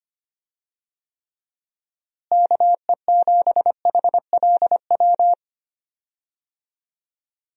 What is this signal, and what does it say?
Morse code, heard as a keyed tone, about 25 wpm, 700 Hz. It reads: KE7HLW
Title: USA Call Signs - 6 Characters - Repeated 2x - 25wpm